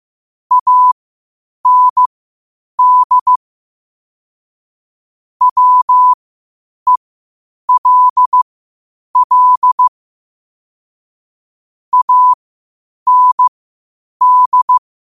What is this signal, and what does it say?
Morse code, heard as a keyed tone, about 15 wpm, spaced as for 8 wpm, 1000 Hz. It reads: AND WELL AND
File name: qotdfiles / 8wpm